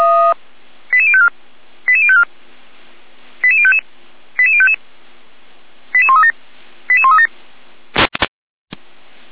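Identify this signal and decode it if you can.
Five-tone selective call (ZVEI), heard as quick runs of five tones, tones of about 70 ms. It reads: 88054 88054 88055 88055 88137 88137
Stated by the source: ZVEI five-tone call received on SDR.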